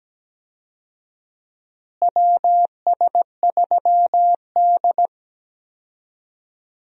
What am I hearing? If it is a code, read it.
Morse code, heard as a keyed tone, about 17 wpm, 700 Hz. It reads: WS3D
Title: USA Call Signs - 4 Characters - Repeated 2x - 17wpm